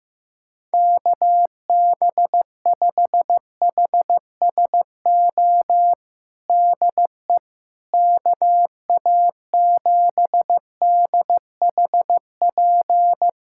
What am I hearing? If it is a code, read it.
Morse code, heard as a keyed tone, about 15 wpm, 700 Hz. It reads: KB5HSO DE KA7DHP
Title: USA Callsign DE Callsign 15wpm